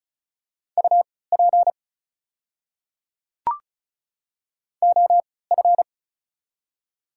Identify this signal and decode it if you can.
Morse code, heard as a keyed tone, about 36 wpm, spaced as for 20 wpm, 700 Hz. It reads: UP OF
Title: 3x Character Spacing - Top 100 Words - 35wpm